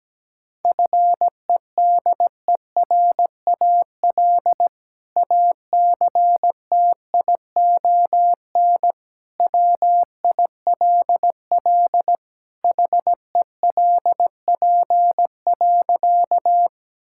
Morse code, heard as a keyed tone, about 17 wpm, 700 Hz. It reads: FEDERAL ACTION WILL HELP.
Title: Sentences from Top 1000 Words - Review Part 5 - 17wpm